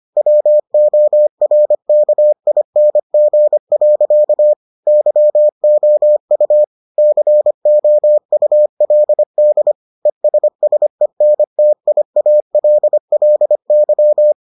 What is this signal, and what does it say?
Morse code, heard as a keyed tone, about 25 wpm, 600 Hz. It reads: WORKING. YOU COULD ESSENTIALLY